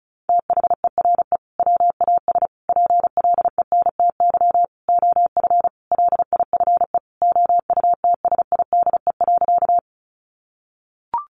Morse code, heard as a keyed tone, about 34 wpm, 700 Hz. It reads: THERE WAS PLENTY OF LIFE OUTSIDE.
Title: Sentences from Top 1300 Words 35wpm